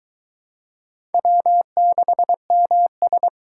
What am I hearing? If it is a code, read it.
Morse code, heard as a keyed tone, about 23 wpm, 700 Hz. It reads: W6MS